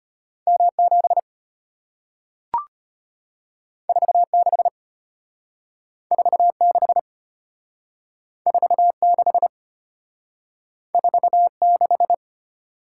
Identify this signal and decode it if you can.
Morse code, heard as a keyed tone, about 35 wpm, 700 Hz. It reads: M7 46 46 46 46